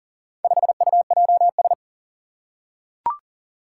Morse code, heard as a keyed tone, about 40 wpm, 700 Hz. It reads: 5UJS
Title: Four Challenging Letter-Number Combinations 40wpm